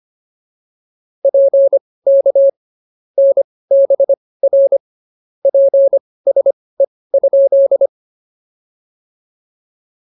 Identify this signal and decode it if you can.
Morse code, heard as a keyed tone, about 25 wpm, spaced as for 18 wpm, 550 Hz. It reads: PK NBR PSE?
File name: POTA207 18